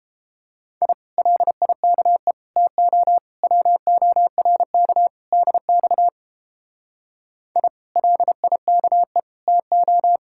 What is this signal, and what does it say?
Morse code, heard as a keyed tone, about 29 wpm, 700 Hz. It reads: I LIKE TO WORK DX I LIKE TO